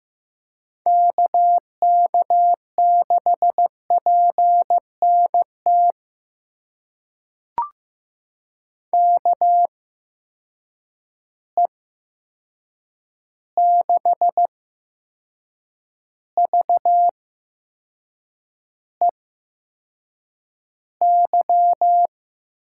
Morse code, heard as a keyed tone, about 15 wpm, spaced as for 6 wpm, 700 Hz. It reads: KK6PNT KE6VEY
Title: USA Call Signs - 6 Characters - 8x Character Spacing - 15wpm